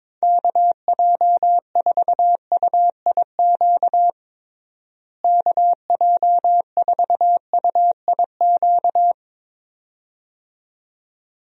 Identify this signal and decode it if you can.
Morse code, heard as a keyed tone, about 21 wpm, 700 Hz. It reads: KJ4UIQ KJ4UIQ
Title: USA Call Signs - 6 Characters - Repeated 2x - 22wpm